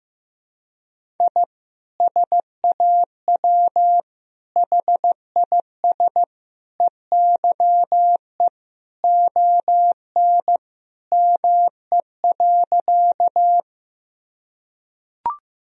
Morse code, heard as a keyed tone, about 15 wpm, 700 Hz. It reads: I SAW HIS EYE ON ME.
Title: Sentences from Top 1000 Words - Review Part 1 - 15wpm